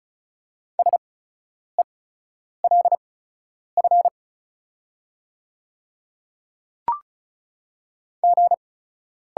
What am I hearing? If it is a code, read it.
Morse code, heard as a keyed tone, about 33 wpm, spaced as for 9 wpm, 690 Hz. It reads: SELF G
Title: 8x Character Spacing - Top 200 Words - 35wpm